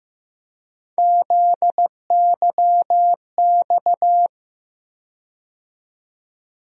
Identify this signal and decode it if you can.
Morse code, heard as a keyed tone, about 15 wpm, 700 Hz. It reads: ZYX
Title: USA Call Sign Postfixes - 3 Letters 15wpm